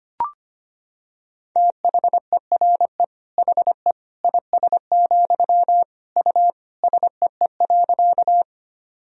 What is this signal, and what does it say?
Morse code, heard as a keyed tone, about 25 wpm, 700 Hz. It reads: THERE HE IS, U SEE.
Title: Sentences from Top 100 Words and Abbreviations 25wpm